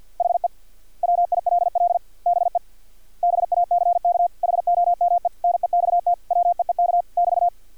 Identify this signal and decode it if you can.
Morse code, heard as a keyed tone, about 50 wpm, 700 Hz. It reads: HE WILL BE BACK HOME NEXT WEEK <BT>
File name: archived / CRSnet